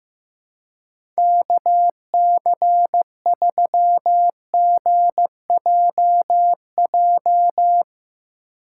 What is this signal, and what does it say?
Morse code, heard as a keyed tone, about 15 wpm, 700 Hz. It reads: KC3GJJ
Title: USA Call Signs - 6 Characters - 6x Character Spacing - 15wpm